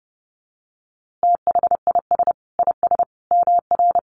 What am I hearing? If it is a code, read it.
Morse code, heard as a keyed tone, about 30 wpm, 700 Hz. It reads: THIS IS MR
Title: A Sound Of Thunder 30wpm